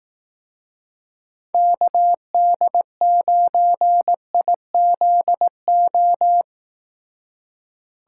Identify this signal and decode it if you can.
Morse code, heard as a keyed tone, about 18 wpm, 700 Hz. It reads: KD9IZO
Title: Speed Racing - USA 2x3 Call Signs 15wpm